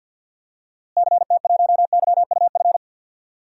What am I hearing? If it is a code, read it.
Morse code, heard as a keyed tone, about 52 wpm, 700 Hz. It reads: CT1YUF
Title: DXCC-100 Call Signs - 6 Characters - Repeated 3x - 50wpm